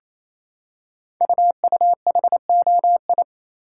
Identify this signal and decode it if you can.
Morse code, heard as a keyed tone, about 27 wpm, 700 Hz. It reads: UUHOI
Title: Code Groups - 5 Letters - 28wpm